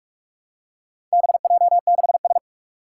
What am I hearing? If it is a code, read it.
Morse code, heard as a keyed tone, about 44 wpm, 700 Hz. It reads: BJ6S
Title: Four Challenging Letter-Number Combinations 45wpm